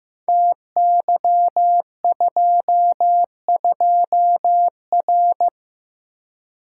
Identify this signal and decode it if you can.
Morse code, heard as a keyed tone, about 15 wpm, 700 Hz. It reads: TY22R
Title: Code Groups - 5 Letter:Numbers - 15wpm